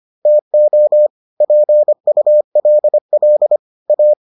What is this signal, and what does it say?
Morse code, heard as a keyed tone, about 25 wpm, 600 Hz. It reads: TO PULL A